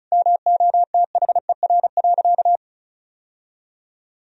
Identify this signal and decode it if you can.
Morse code, heard as a keyed tone, about 34 wpm, 700 Hz. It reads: MOTHER.
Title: Sentences from Top 1000 Words - Review Part 4 - 35wpm